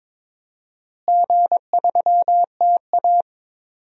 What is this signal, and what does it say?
Morse code, heard as a keyed tone, about 22 wpm, 700 Hz. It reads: G3TA
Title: DXCC-100 Call Signs - 4 Characters - Repeated 3x - 22wpm